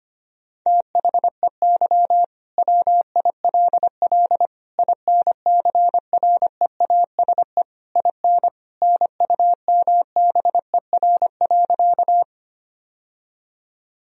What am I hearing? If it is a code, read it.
Morse code, heard as a keyed tone, about 25 wpm, 700 Hz. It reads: THEY WILL INCREASE IN NUMBER.
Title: Sentences from Top 1000 Words - Review Part 4 - 25wpm